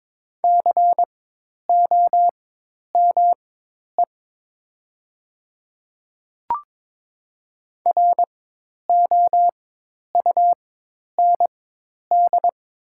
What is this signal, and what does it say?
Morse code, heard as a keyed tone, about 22 wpm, spaced as for 10 wpm, 700 Hz. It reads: COME ROUND
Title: 4x Character Spacing - Top 200 Words - 22wpm